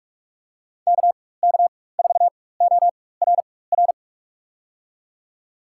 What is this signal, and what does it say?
Morse code, heard as a keyed tone, about 45 wpm, spaced as for 21 wpm, 700 Hz. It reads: KK4ORR
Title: USA Call Signs - 6 Characters - 4x Character Spacing - 45wpm